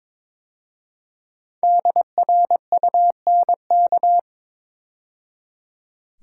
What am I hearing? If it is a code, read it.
Morse code, heard as a keyed tone, about 23 wpm, 700 Hz. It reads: DRUNK
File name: 5 Letter Words - First 3 Letters Given - 22wpm